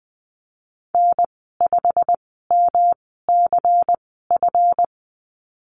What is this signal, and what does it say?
Morse code, heard as a keyed tone, about 20 wpm, spaced as for 15 wpm, 700 Hz. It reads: N5MCF
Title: 147 Lesson 24g - Letter y - Call Signs 20wpm